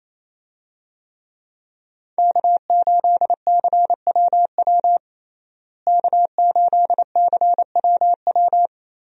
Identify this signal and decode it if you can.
Morse code, heard as a keyed tone, about 27 wpm, 700 Hz. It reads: K8CWW K8CWW
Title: USA Call Signs - 5 Characters - Repeated 2x - 28wpm